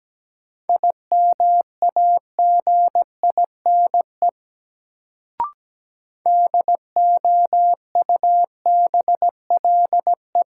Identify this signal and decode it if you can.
Morse code, heard as a keyed tone, about 17 wpm, 700 Hz. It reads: IMAGINE DOUBLE
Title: Sentences from Top 1400 Words - New Word Drill - 17wpm